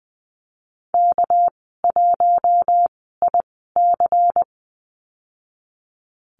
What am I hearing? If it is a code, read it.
Morse code, heard as a keyed tone, about 20 wpm, spaced as for 14 wpm, 700 Hz. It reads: K1IC
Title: 207 Lesson 33g - Letter j - Call Signs 20wpm